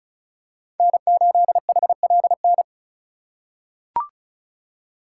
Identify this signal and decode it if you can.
Morse code, heard as a keyed tone, about 36 wpm, 700 Hz. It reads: N8HLN
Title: USA Call Signs - 4x Character Spacing - 35wpm